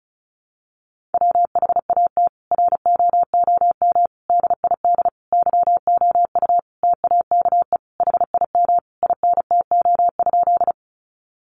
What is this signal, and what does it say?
Morse code, heard as a keyed tone, about 34 wpm, 700 Hz. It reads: WHAT ROOM DID YOU TAKE HIM INTO?
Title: Sentences from Top 600 Words 35wpm